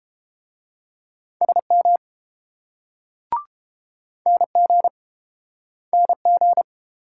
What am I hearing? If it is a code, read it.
Morse code, heard as a keyed tone, about 31 wpm, 700 Hz. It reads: SM NG NG